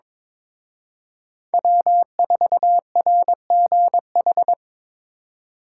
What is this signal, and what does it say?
Morse code, heard as a keyed tone, about 25 wpm, 700 Hz. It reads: W4RGH
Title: Speed Racing - USA 1x3 Call Signs 22wpm